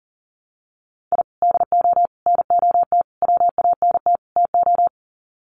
Morse code, heard as a keyed tone, about 40 wpm, 700 Hz. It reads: I DO NOT WANT TO
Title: Sets of 5 Words 40wpm